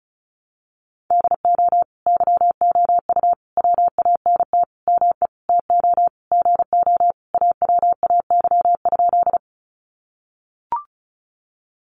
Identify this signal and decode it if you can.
Morse code, heard as a keyed tone, about 34 wpm, 700 Hz. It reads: DO YOU WANT ME TO GO AWAY?
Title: Sentences from Top 600 Words 35wpm